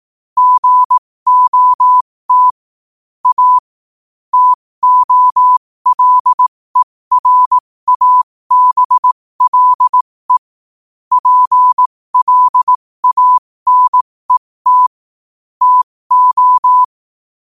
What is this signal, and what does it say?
Morse code, heard as a keyed tone, about 18 wpm, spaced as for 15 wpm, 1000 Hz. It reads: GOT A TOLERABLE PLANET TO
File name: qotdfiles / 15wpm